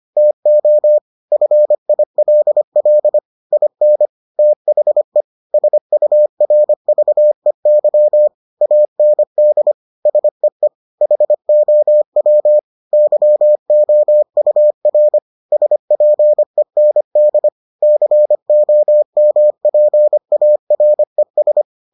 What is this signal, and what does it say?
Morse code, heard as a keyed tone, about 25 wpm, 600 Hz. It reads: TO FILL IN THE SURVEY AND SEE HOW YOUR SPEND COMPARES